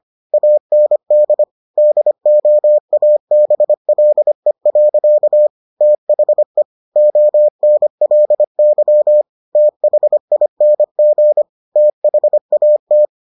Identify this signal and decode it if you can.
Morse code, heard as a keyed tone, about 25 wpm, 600 Hz. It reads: AND DOABLE. THE ONLY THING THAT